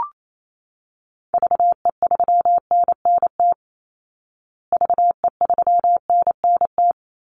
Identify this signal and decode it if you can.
Morse code, heard as a keyed tone, about 28 wpm, 700 Hz. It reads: VE3NNT VE3NNT
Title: Call Signs Repeated 3x - 28wpm